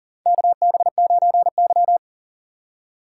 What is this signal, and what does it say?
Morse code, heard as a keyed tone, about 40 wpm, 700 Hz. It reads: KB9Y
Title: USA Call Signs - 4 Characters - 6x Character Spacing - 40wpm